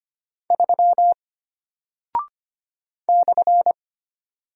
Morse code, heard as a keyed tone, about 25 wpm, 700 Hz. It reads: 3 /
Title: Single Character (Letter:Number:Punctuation:Prosign) Alt 25wpm